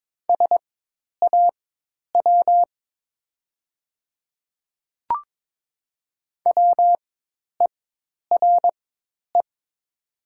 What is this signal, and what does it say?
Morse code, heard as a keyed tone, about 22 wpm, spaced as for 10 wpm, 700 Hz. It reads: SAW WERE
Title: 4x Character Spacing - Top 300 Words - 22wpm